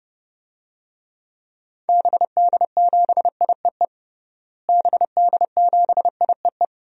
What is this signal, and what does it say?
Morse code, heard as a keyed tone, about 30 wpm, 700 Hz. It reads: BD7IEE BD7IEE
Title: DXCC-100 Call Signs - 6 Characters - Repeated 2x - 30wpm